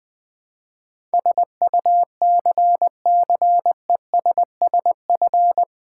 Morse code, heard as a keyed tone, about 20 wpm, 700 Hz. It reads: SUCCESSF
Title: Build A Word - 4 Letters At A Time - 10-Letter Words - 20wpm